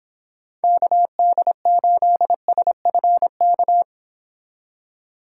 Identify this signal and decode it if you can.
Morse code, heard as a keyed tone, about 31 wpm, 700 Hz. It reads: KD8SFK